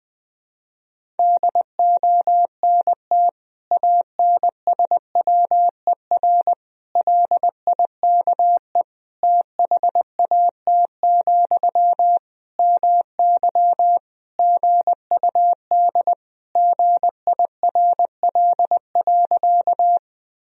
Morse code, heard as a keyed tone, about 20 wpm, 700 Hz. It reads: DONT ANSWER LIKE THAT, MY GUD GIRL.
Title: Sentences from Top 400 Words and Abbreviations 20wpm